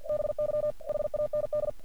Morse code, heard as a keyed tone, about 46 wpm, 600 Hz. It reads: BY HAND
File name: archived / CRSnet